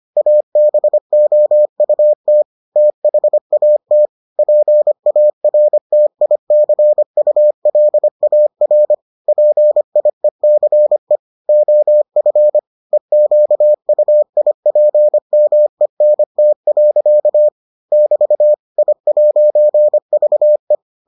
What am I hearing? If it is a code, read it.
Morse code, heard as a keyed tone, about 25 wpm, 600 Hz. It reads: ABOUT THAT PARTICULAR PIECE OF EQUIPMENT. <BT> I'VE